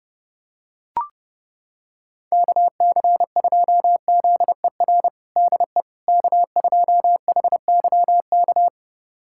Morse code, heard as a keyed tone, about 30 wpm, 700 Hz. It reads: KC2ZER DE K2HYK
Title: USA Callsign DE Callsign 30wpm